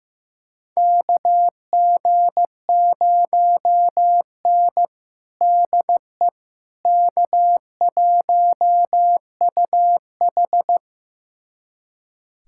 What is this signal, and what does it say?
Morse code, heard as a keyed tone, about 15 wpm, 700 Hz. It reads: KG0N DE K1UH
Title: USA Callsign DE Callsign - 4 Characters - 15wpm